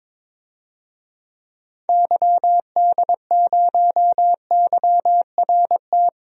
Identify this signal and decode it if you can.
Morse code, heard as a keyed tone, about 22 wpm, 700 Hz. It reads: YD0YRT
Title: DXCC-100 Call Signs - 6 Characters - Repeated 2x - 22wpm